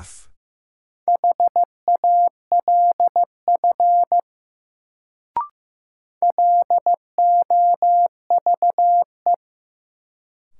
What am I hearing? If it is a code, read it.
Morse code, heard as a keyed tone, about 16 wpm, 700 Hz. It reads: HALF LOVE
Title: Sentences from Top 400 Words - New Word Drill - 15wpm